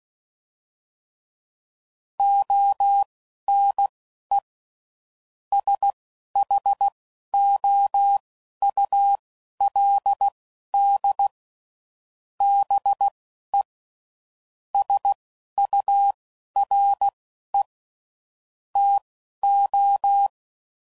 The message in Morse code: ONE SHOULD BE SURE TO

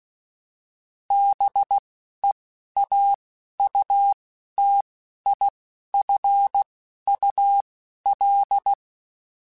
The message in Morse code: BEAUTIFUL

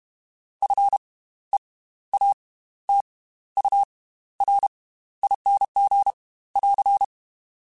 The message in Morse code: FEATURING <AR>

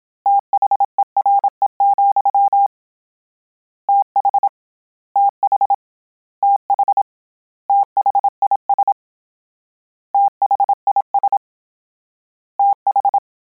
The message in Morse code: THERE, TH TH TH THIS THIS TH

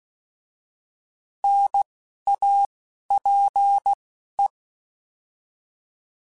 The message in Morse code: NAPE

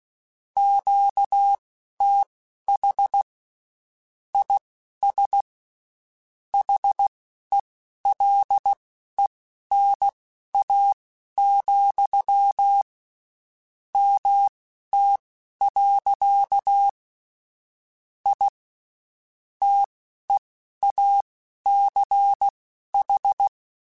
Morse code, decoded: QTH IS HELENA, MT. I TEACH

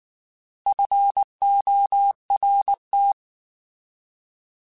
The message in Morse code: FORT

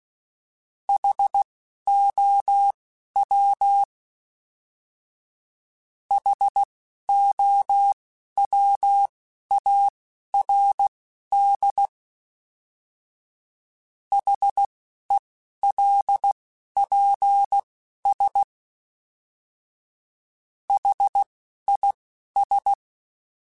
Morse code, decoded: HOW HOWARD HELPS HIS

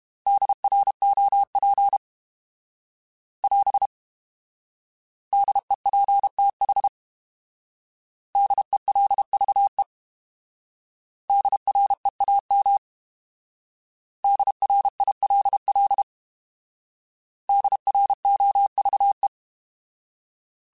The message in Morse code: DROP <AS> DEPTH DELVE DREAM DRILL DROVE

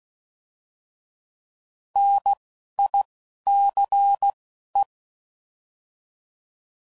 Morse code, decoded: NICE

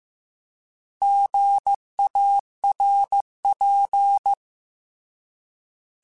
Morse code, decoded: GARP

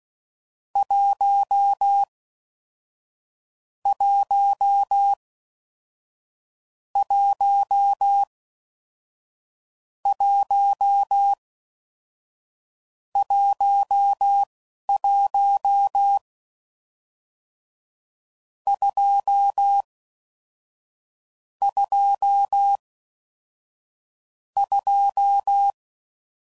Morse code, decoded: 1 1 1 1 11 2 2 2

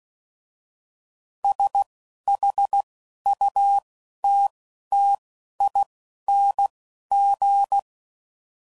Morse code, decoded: S H U T T I N G

